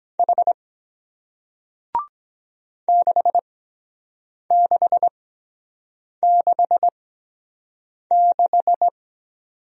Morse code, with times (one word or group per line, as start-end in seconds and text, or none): 0.2-0.5 H
2.9-8.9 6666